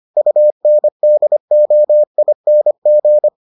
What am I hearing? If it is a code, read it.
Morse code, heard as a keyed tone, about 25 wpm, 600 Hz. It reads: UNDOING